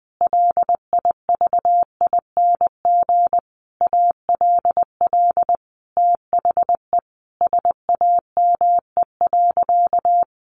Morse code, heard as a keyed tone, about 19 wpm, 700 Hz. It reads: LIVING ALL THE SAME.
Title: Sentences from Top 1500 Words 20wpm